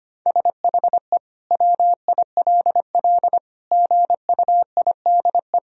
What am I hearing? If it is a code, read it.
Morse code, heard as a keyed tone, about 25 wpm, 700 Hz. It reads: SHE WILL GUIDE
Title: Sentences from Top 1000 Words - Review Part 3 - 25wpm